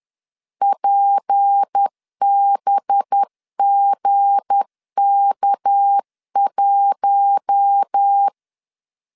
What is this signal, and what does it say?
Morse code, heard as a keyed tone, about 10 wpm, 800 Hz. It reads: PBGK1
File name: LEZIONI / 9 2 10wpm 2736